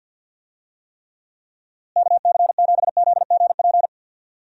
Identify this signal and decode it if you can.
Morse code, heard as a keyed tone, about 52 wpm, 700 Hz. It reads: KC7ZGP
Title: USA Call Signs - 6 Characters - Repeated 2x - 50wpm